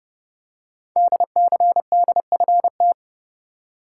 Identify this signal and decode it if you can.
Morse code, heard as a keyed tone, about 30 wpm, 700 Hz. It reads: DCDFT
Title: Code Groups - 5 Letters - 30wpm